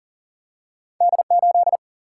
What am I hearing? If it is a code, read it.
Morse code, heard as a keyed tone, about 40 wpm, 700 Hz. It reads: D8
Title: Single-Letter Single-Number 40wpm